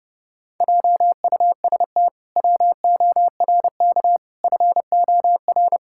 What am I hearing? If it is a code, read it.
Morse code, heard as a keyed tone, about 30 wpm, 700 Hz. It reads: JUST WORK FOR
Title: Pangram Challenge v2 - 30wpm